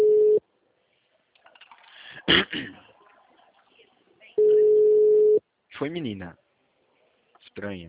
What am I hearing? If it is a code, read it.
Ringback tone (Europe, 425 Hz), with speech.